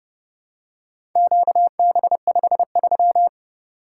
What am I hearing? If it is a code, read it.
Morse code, heard as a keyed tone, about 30 wpm, 700 Hz. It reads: QB53